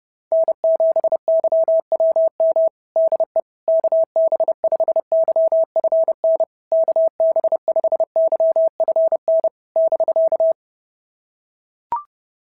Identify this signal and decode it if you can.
Morse code, heard as a keyed tone, about 30 wpm, 660 Hz. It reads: N7YWM DE KB5YFN KB5YFN <BK>
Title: SOTA QSO Practice 30wpm